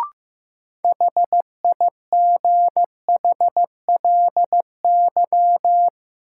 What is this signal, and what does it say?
Morse code, heard as a keyed tone, about 15 wpm, 700 Hz. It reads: HIGHLY
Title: Sentences from Top 1100 Words - New Word Drill Alt - 15wpm